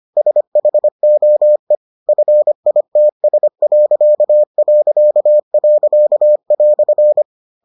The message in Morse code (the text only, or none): SHOE FITS..."